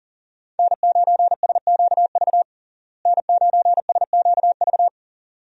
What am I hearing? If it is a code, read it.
Morse code, heard as a keyed tone, about 40 wpm, 700 Hz. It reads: N9SQV N9SQV
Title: USA Call Signs - 5 Characters - Repeated 2x - 40wpm